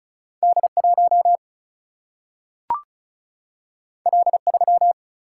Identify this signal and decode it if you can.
Morse code, heard as a keyed tone, about 36 wpm, 700 Hz. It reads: D1 L3
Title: Single-Letter Single-Number 35wpm